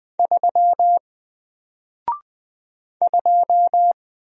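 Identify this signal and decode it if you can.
Morse code, heard as a keyed tone, about 20 wpm, 700 Hz. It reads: 3 2
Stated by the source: Single letter and number Morse drill at 20 wpm, from N4LCD.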